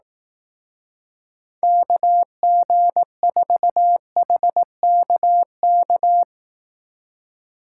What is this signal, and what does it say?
Morse code, heard as a keyed tone, about 18 wpm, 700 Hz. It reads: KG4HKK
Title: Speed Racing - USA 2x3 Call Signs 15wpm